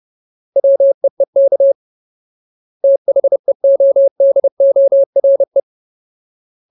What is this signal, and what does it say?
Morse code, heard as a keyed tone, about 30 wpm, 550 Hz. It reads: WEEK THEODORE
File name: Ultimate Word List / 30WPM Ultimate